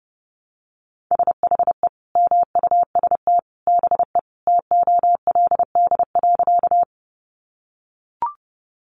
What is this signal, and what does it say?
Morse code, heard as a keyed tone, about 30 wpm, 700 Hz. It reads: SHE MUST BE TOLD.
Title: Sentences from Top 400 Words 30wpm